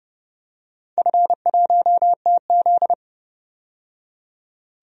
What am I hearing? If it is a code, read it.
Morse code, heard as a keyed tone, about 30 wpm, 700 Hz. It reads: F1TZ
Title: DXCC-100 Call Signs - 4 Characters - 30wpm